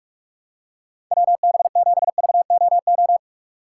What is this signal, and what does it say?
Morse code, heard as a keyed tone, about 43 wpm, 700 Hz. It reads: WB7VOO
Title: USA Call Signs - 4x Character Spacing - 45wpm